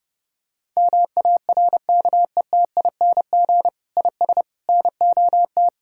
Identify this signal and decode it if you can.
Morse code, heard as a keyed tone, about 31 wpm, 700 Hz. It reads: MARKETING IS NOT
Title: Sentences from Top 1000 Words - Review Part 5 - 30wpm